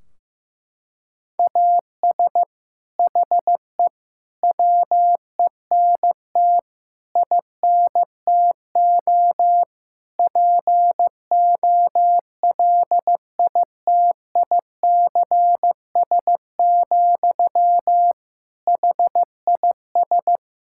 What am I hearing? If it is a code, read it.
Morse code, heard as a keyed tone, about 15 wpm, 700 Hz. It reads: AS HE WENT INTO POLITICS, HIS
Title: Sentences from Top 1600 Words 15wpm